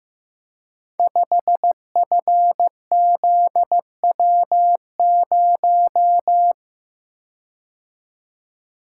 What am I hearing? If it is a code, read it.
Morse code, heard as a keyed tone, about 15 wpm, 700 Hz. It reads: 5FZW0
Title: Code Groups - 5 Letter:Numbers - 15wpm